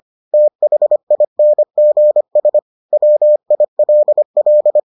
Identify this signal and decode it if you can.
Morse code, heard as a keyed tone, about 25 wpm, 600 Hz. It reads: THINGS WILL